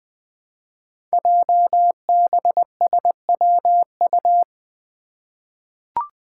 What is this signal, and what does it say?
Morse code, heard as a keyed tone, about 20 wpm, 700 Hz. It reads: JBSWU
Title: Code Groups - 5 Letter:Numbers - 20wpm